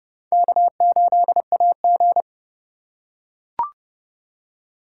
Morse code, heard as a keyed tone, about 29 wpm, 700 Hz. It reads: K8AG